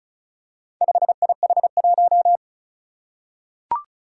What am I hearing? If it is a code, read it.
Morse code, heard as a keyed tone, about 36 wpm, 690 Hz. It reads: 5IH1
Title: Four Challenging Letter-Number Combinations 35wpm